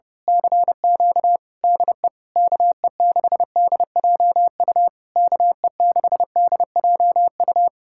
Morse code, heard as a keyed tone, about 28 wpm, 700 Hz. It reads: CQ DE KE6DJU KE6DJU
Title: Speed-Racing Common QSO Phrases - 30wpm